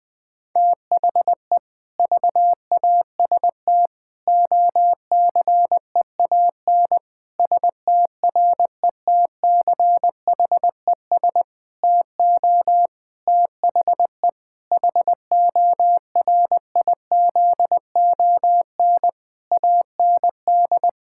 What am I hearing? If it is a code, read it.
Morse code, heard as a keyed tone, about 21 wpm, 700 Hz. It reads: THE VAST OCEAN STRETCHES TO THE HORIZON AND
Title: Difficult Sentences with Hints - 20wpm